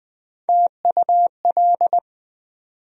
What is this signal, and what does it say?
Morse code, heard as a keyed tone, about 20 wpm, 700 Hz. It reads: TUL
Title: USA Call Sign Postfixes - 3 Letters 20wpm